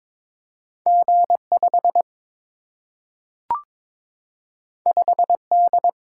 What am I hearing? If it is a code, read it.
Morse code, heard as a keyed tone, about 22 wpm, 700 Hz. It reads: G 5 5 D
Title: Single-Letter Single-Number 22wpm